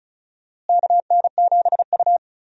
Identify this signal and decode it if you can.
Morse code, heard as a keyed tone, about 36 wpm, 700 Hz. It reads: KN7U
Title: USA Call Signs - 4 Characters - Repeated 3x - 35wpm